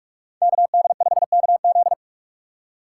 Morse code, heard as a keyed tone, about 46 wpm, 700 Hz. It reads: KD5KZ
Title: USA Call Signs - 5 Characters - 3x Character Spacing - 45wpm